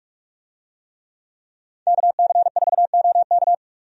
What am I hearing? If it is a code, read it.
Morse code, heard as a keyed tone, about 41 wpm, 700 Hz. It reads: KC4OK
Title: USA Call Signs - 5 Characters - 45wpm